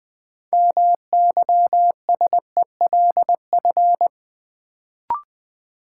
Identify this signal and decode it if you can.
Morse code, heard as a keyed tone, about 21 wpm, 700 Hz. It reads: MYSELF
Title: Sentences from Top 1300 Words - New Word Drill - 20wpm